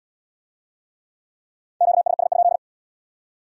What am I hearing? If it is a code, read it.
Morse code, heard as a keyed tone, about 71 wpm, 700 Hz. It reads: QSS?